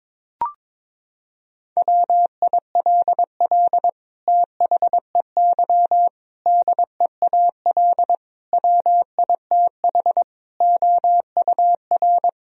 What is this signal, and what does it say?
Morse code, heard as a keyed tone, about 22 wpm, 700 Hz. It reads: WILL THEY DEAL WITH OUR
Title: Sentences from Top 1000 Words - Review Part 4 - 22wpm